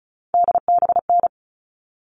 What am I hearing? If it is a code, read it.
Morse code, heard as a keyed tone, about 34 wpm, 700 Hz. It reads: DBN
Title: All Cut Numbers - 3-Digit 35wpm